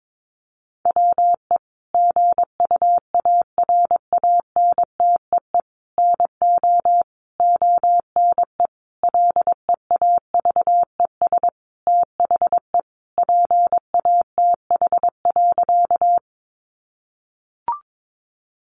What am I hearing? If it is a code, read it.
Morse code, heard as a keyed tone, about 22 wpm, 700 Hz. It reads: WE GUARANTEE NO ONE LEAVES THE PATH.
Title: A Sound Of Thunder 22wpm